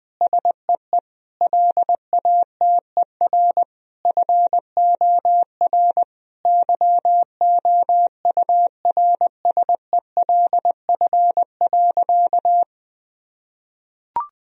Morse code, heard as a keyed tone, about 20 wpm, 700 Hz. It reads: SEE LATER FOR YOURSELF.
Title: Sentences from Top 1000 Words - Review Part 4 - 20wpm